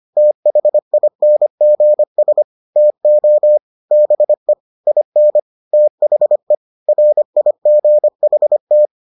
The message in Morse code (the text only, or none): THINGS TO BE IN THE RIGHT